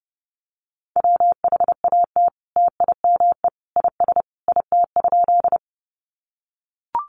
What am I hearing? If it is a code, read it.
Morse code, heard as a keyed tone, about 30 wpm, 700 Hz. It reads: WHAT TIME IS IT?
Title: Sentences from Top 100 Words 30wpm